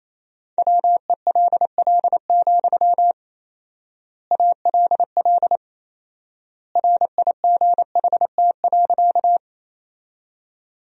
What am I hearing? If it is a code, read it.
Morse code, heard as a keyed tone, about 28 wpm, 700 Hz. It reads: WELL, ALL RIGHT.
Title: 4x Word-Spacing - Sentences from Top 200 Words - 28wpm